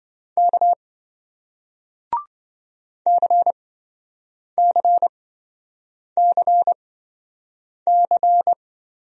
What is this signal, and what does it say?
Morse code, heard as a keyed tone, about 27 wpm, spaced as for 7 wpm, 700 Hz. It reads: K CCCC